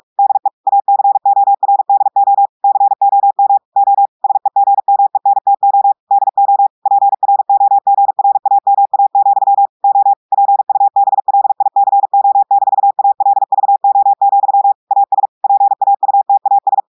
Morse code, heard as a keyed tone, about 46 wpm, 800 Hz. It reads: DE ACORDO COM O SEGMENTO DO PROGRAMA, O PUBLICO-ALVO, AS PAUTAS
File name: audios / portugais / portugais texte en clair 2 vitesse 45 mots minutes